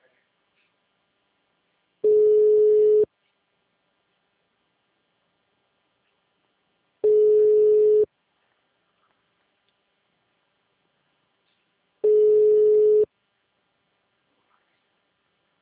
Ringback tone (Europe, 425 Hz).
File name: out- -2011- - - .6138